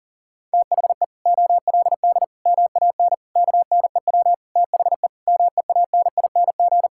Morse code, heard as a keyed tone, about 40 wpm, 700 Hz. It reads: THE OLD MAN KNEW THE MEANING
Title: Sentences from Top 1000 Words - Review Part 6 - 40wpm